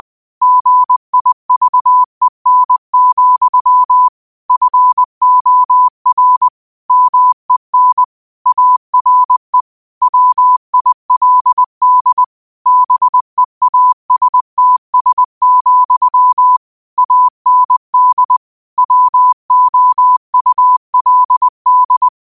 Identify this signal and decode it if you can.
Morse code, heard as a keyed tone, about 20 wpm, 1000 Hz. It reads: GIVEN, FOR MEN ARE WILD BEASTS, AND WOULD